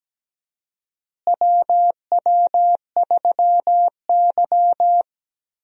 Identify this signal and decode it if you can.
Morse code, heard as a keyed tone, about 17 wpm, 700 Hz. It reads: WW3Y
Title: USA Call Signs - 4 Characters - Repeated 3x - 17wpm